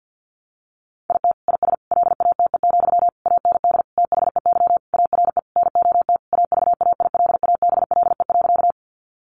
Morse code, heard as a keyed tone, about 50 wpm, 700 Hz. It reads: IT IS LATE, AND THEY ARE NOT AVAILABLE.
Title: Sentences from Top 600 Words 50wpm